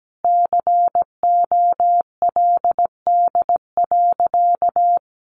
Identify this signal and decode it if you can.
Morse code, heard as a keyed tone, about 17 wpm, 700 Hz. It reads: COLD.
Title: Sentences from Top 1100 Words 17wpm